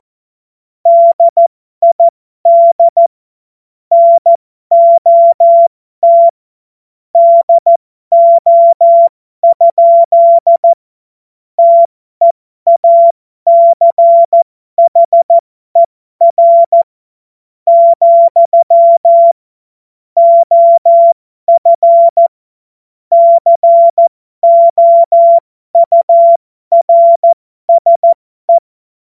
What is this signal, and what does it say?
Morse code, heard as a keyed tone, about 14 wpm, spaced as for 12 wpm, 680 Hz. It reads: DID NOT DO? TEACHER, OF COURSE
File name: Morsekurs / Witze